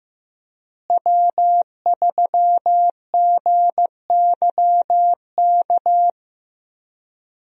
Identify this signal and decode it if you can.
Morse code, heard as a keyed tone, about 15 wpm, 700 Hz. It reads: W3GYK
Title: USA Call Signs - 5 Characters - Repeated 3x - 15wpm